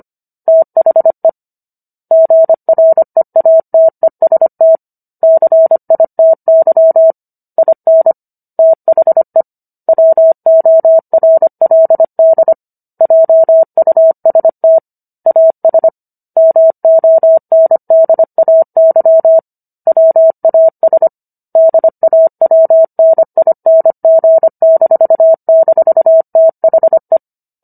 Morse code, heard as a keyed tone, about 25 wpm, 650 Hz. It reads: THE GREATEST CITY IN THE WORLD JUST AS MONDAY WAS DAWNING--THE